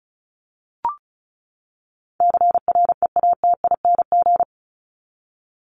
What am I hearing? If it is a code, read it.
Morse code, heard as a keyed tone, about 34 wpm, 700 Hz. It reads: CREATING
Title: ING Suffix - Top 100 Words with Base Words 35wpm